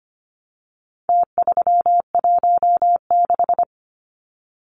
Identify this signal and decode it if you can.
Morse code, heard as a keyed tone, about 25 wpm, 700 Hz. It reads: T316
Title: Numbers - Contest Style - 4-Digit 25wpm